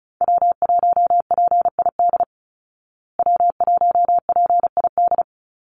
Morse code, heard as a keyed tone, about 34 wpm, 700 Hz. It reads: W1PID W1PID
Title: Call Signs Repeated 3x - 35wpm